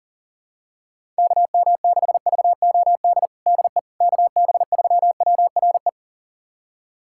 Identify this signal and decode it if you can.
Morse code, heard as a keyed tone, about 40 wpm, 700 Hz. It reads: KM6VOD DE KB3WRE
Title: USA Callsign DE Callsign 40wpm